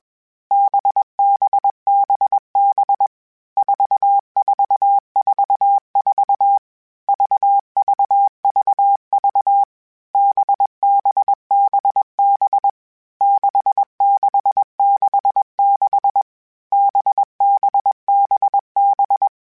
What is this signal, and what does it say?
Morse code, heard as a keyed tone, about 21 wpm, 800 Hz. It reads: BBBB 4444 VVVV BBBB 6666 BBBB
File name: Basic Sets / Basic Letter Groups Collective7 / VB46@20